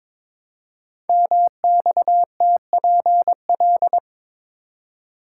Morse code, heard as a keyed tone, about 22 wpm, 700 Hz. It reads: MXTPL
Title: Code Groups - 5 Letters - 22wpm